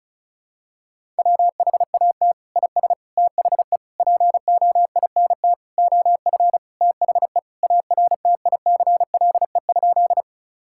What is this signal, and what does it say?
Morse code, heard as a keyed tone, about 36 wpm, 700 Hz. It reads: WHAT IS THE POINT OF THE ARTICLE?
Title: Sentences from Top 1000 Words - Review Part 5 - 35wpm